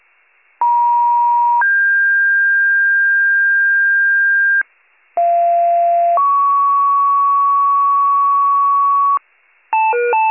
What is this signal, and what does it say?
Two-tone page, heard as a one-second tone, then a three-second tone, about 950 then 1600 Hz, about 690 then 1120 Hz, with noise. It is heard from 0.6 to 9.2 s.